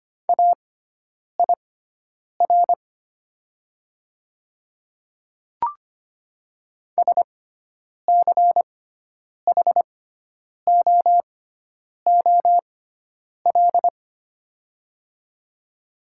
Morse code, heard as a keyed tone, about 25 wpm, spaced as for 9 wpm, 700 Hz. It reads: AIR SCHOOL
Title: 6x Character Spacing - Top 300 Words - 25wpm